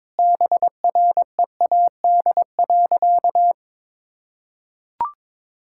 Morse code, heard as a keyed tone, about 21 wpm, 700 Hz. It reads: BREAD.
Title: Sentences from Top 1600 Words 22wpm